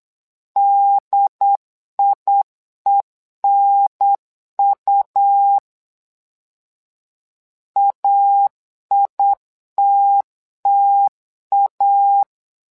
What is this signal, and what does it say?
Morse code, heard as a keyed tone, about 8 wpm, 800 Hz. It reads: DIENU AITTA
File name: LEZIONI / 2 UHDN / 2 UHDN 8wpm2232